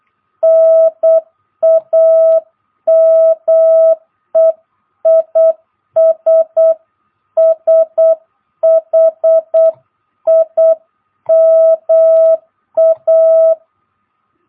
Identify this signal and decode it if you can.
Morse code, heard as a keyed tone, about 8 wpm, 660 Hz. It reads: NAME IS SHIMA
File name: jh8cbh / 2022cw